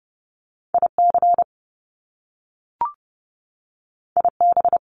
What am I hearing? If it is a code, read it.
Morse code, heard as a keyed tone, about 30 wpm, 700 Hz. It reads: IC IB
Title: Top Two Letter Combinations in Words 30wpm